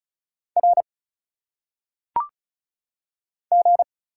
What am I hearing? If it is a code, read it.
Morse code, heard as a keyed tone, about 36 wpm, 700 Hz. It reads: R G
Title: Single Letters - Phonetic Pronunciation 35wpm